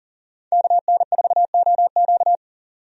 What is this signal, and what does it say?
Morse code, heard as a keyed tone, about 40 wpm, 700 Hz. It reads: KN4OQ
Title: USA Call Signs - 5 Characters - 40wpm